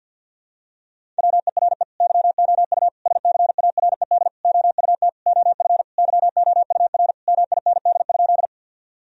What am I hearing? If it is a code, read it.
Morse code, heard as a keyed tone, about 51 wpm, 700 Hz. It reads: WERE YOU SCARED OUT OF YOUR MIND?
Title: Sentences from Top 1000 Words - Review Part 4 - 50wpm